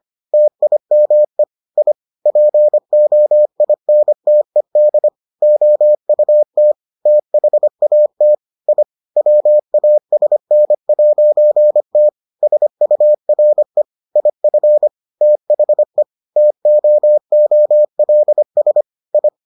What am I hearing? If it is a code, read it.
Morse code, heard as a keyed tone, about 25 wpm, 600 Hz. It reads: TIME I POINTED OUT THAT I WASN'T SURE IF THE TOOLS I